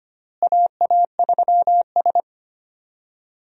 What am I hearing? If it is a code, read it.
Morse code, heard as a keyed tone, about 25 wpm, 700 Hz. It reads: AA3S
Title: USA Call Signs - 4 Characters - Repeated 2x - 25wpm